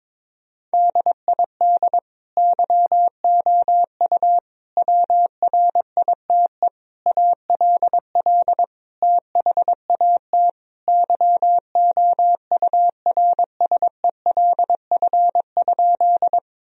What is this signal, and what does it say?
Morse code, heard as a keyed tone, about 22 wpm, 700 Hz. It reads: DID YOU WRITE ALL THAT YOURSELF?
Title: Sentences from Top 1000 Words - Review Part 1 - 22wpm